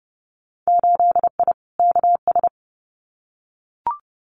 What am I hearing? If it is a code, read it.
Morse code, heard as a keyed tone, about 30 wpm, 700 Hz. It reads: 8I KS
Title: Winter Field Day Exchanges Encore 30wpm